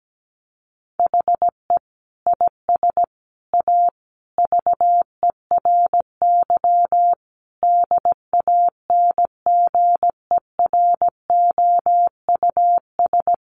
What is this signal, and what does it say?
Morse code, heard as a keyed tone, about 17 wpm, 700 Hz. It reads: HE IS A VERY DANGEROUS
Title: Sentences from Top 1200 Words 17wpm